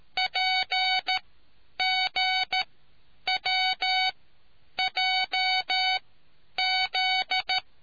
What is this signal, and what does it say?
Morse code, heard as a keyed tone, about 13 wpm, spaced as for 9 wpm, 750 Hz. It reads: PGWJZ